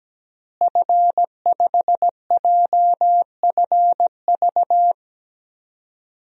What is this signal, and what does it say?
Morse code, heard as a keyed tone, about 17 wpm, 700 Hz. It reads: F5JFV
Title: DXCC-100 Call Signs - 5 Characters - Repeated 2x - 17wpm